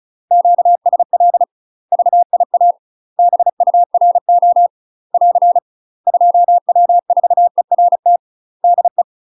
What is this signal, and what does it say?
Morse code, heard as a keyed tone, about 34 wpm, 700 Hz. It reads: QSL VIA BURO <AR> 2W4ERT DE
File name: UK MASTER QSOs TRAINING / 35WPM / UK MASTER QSOs 0024